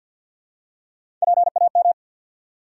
Morse code, heard as a keyed tone, about 52 wpm, 700 Hz. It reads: PAM
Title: USA Common Names - Max 4 Letters - 50wpm